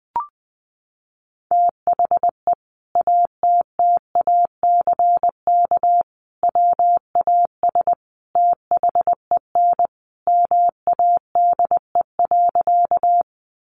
Morse code, heard as a keyed tone, about 20 wpm, 700 Hz. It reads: THE ATTACK WAS THEN MADE.
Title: Sentences from Top 1500 Words 20wpm